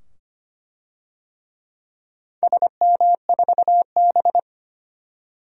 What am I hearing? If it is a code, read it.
Morse code, heard as a keyed tone, about 25 wpm, 700 Hz. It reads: SM4B
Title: DXCC-100 Call Signs - 4 Characters - 25wpm